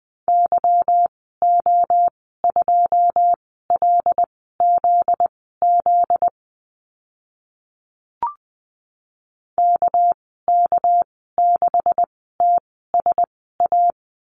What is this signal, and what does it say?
Morse code, heard as a keyed tone, about 19 wpm, spaced as for 14 wpm, 700 Hz. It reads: YO2LZZ KK6TSA
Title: 245 Lesson 39g - Letter z - Call Signs 20wpm